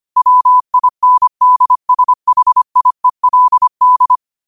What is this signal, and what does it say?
Morse code, heard as a keyed tone, about 25 wpm, 1000 Hz. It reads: WINDSHIELD
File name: qotdfiles / 25wpm